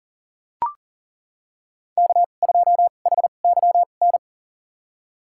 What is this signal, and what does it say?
Morse code, heard as a keyed tone, about 40 wpm, spaced as for 29 wpm, 700 Hz. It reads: K2HYN
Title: USA Call Signs - 2x Character Spacing - 40wpm